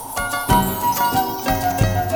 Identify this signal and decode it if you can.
There is no code.